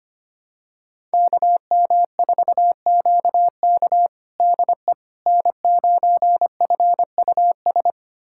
Morse code, heard as a keyed tone, about 25 wpm, 700 Hz. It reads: KM4QK DE N9FUS
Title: USA Callsign DE Callsign - 5 Characters - 25wpm